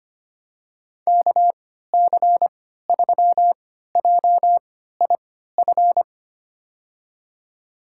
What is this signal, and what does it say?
Morse code, heard as a keyed tone, about 25 wpm, spaced as for 14 wpm, 700 Hz. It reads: KC3JIF
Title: USA Call Signs - 6 Characters - 3x Character Spacing - 25wpm